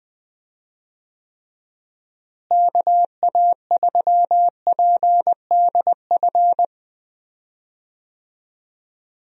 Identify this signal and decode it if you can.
Morse code, heard as a keyed tone, about 20 wpm, 700 Hz. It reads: KA3PDF
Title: USA Call Signs - 6 Characters - Repeated 3x - 20wpm